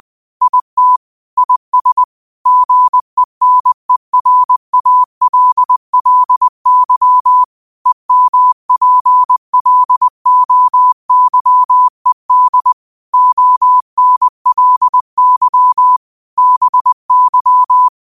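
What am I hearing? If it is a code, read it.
Morse code, heard as a keyed tone, about 20 wpm, 1000 Hz. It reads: IT IS GENERALLY EMPLOYED ONLY BY
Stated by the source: Morse code quote of the day at 20 wpm.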